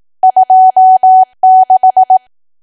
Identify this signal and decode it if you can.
Morse code, heard as a keyed tone, about 18 wpm, 750 Hz. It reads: 26
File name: cwa / W1AW-Level2 / 18WPM